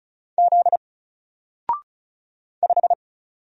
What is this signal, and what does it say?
Morse code, heard as a keyed tone, about 34 wpm, 700 Hz. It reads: Z 5
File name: Single-Letter-Number-35wpm